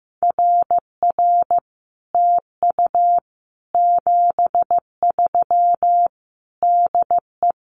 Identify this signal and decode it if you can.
Morse code, heard as a keyed tone, about 15 wpm, 700 Hz. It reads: RR TU 73 DE
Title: Common QSO Phrases Encore 15wpm